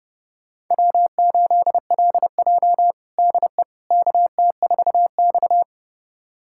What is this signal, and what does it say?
Morse code, heard as a keyed tone, about 30 wpm, 700 Hz. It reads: W8LJ DE KT4X
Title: USA Callsign DE Callsign - 4 Characters - 30wpm